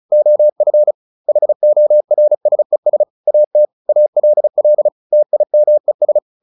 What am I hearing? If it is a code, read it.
Morse code, heard as a keyed tone, about 34 wpm, 600 Hz. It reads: OF HORSES AT ALL TIMES
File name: the old man and the sea / the old man and the sea 1 35wpm